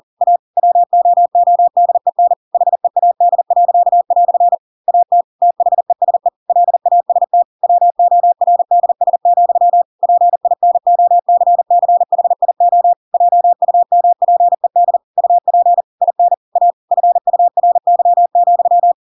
Morse code, heard as a keyed tone, about 40 wpm, 700 Hz. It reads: A WOODEN HEAD." AT THESE LAST WORDS, PINOCCHIO JUMPED UP IN A FURY,
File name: Pinocchio / Pinocchio ch4 40 40 ik1hge